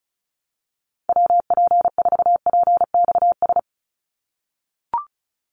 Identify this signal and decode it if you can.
Morse code, heard as a keyed tone, about 34 wpm, 700 Hz. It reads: WP4PXS
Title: USA Call Signs 35wpm